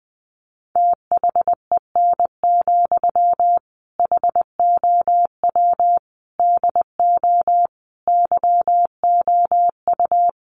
Morse code, heard as a keyed tone, about 19 wpm, 700 Hz. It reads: THEN, HOW DO YOU
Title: Sentences from Top 100 Words 20wpm